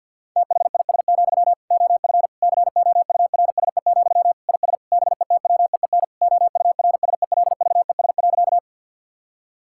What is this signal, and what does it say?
Morse code, heard as a keyed tone, about 50 wpm, 700 Hz. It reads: THIS, OF COURSE, IS BETWEEN OURSELVES.
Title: Sentences from Top 1600 Words 50wpm